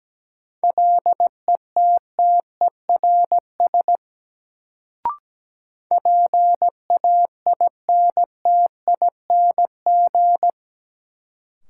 Morse code, heard as a keyed tone, about 17 wpm, 700 Hz. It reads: LETTERS PAINTING
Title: Sentences from Top 1000 Words - Review Part 6 - New Word Drill - 17wpm